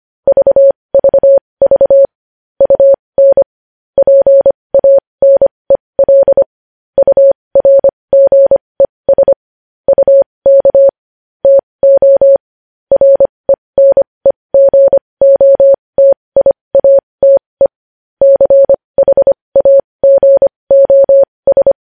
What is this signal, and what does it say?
Morse code, heard as a keyed tone, about 25 wpm, spaced as for 20 wpm, 550 Hz. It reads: VVV UN PANEL URGES UK TO RENEGOTIATE CHAGOS
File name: podcast / News headlines-1x-25x20